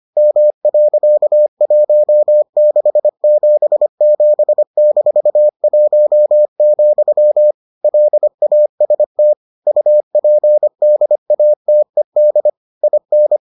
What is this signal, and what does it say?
Morse code, heard as a keyed tone, about 25 wpm, 600 Hz. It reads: M.1677-1, LAST UPDATED IN